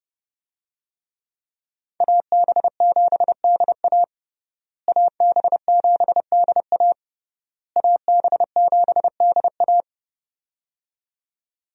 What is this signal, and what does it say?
Morse code, heard as a keyed tone, about 30 wpm, 700 Hz. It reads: AB7DA AB7DA AB7DA
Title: USA Call Signs - 5 Characters - Repeated 3x - 30wpm